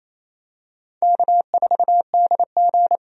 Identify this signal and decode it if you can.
Morse code, heard as a keyed tone, about 28 wpm, 700 Hz. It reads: K4DG
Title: USA Call Signs - 4 Characters - 28wpm